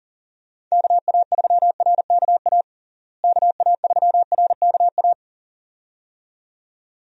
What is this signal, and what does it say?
Morse code, heard as a keyed tone, about 40 wpm, 700 Hz. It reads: KA3RKA KA3RKA
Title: USA Call Signs - 6 Characters - Repeated 2x - 40wpm